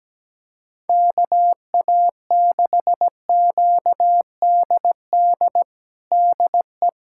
Morse code, heard as a keyed tone, about 17 wpm, 700 Hz. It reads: KA6QDD DE
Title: USA Callsign DE Callsign 17wpm